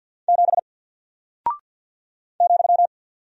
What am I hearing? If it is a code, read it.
Morse code, heard as a keyed tone, about 48 wpm, 700 Hz. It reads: 7 ,
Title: Single Character (Letter:Number:Punctuation:Prosign) Alt 50wpm